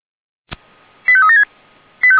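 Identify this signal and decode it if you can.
Five-tone selective call (ZVEI), heard as quick runs of five tones, tones of about 70 ms. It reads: 85376
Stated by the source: ZVEI five-tone call 89485 received on SDR by Feuerwehr Orlamünde.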